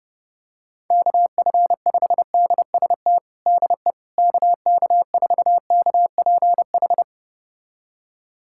Morse code, heard as a keyed tone, about 32 wpm, 690 Hz. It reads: KF5DST DE KK4KPH
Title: USA Callsign DE Callsign 30wpm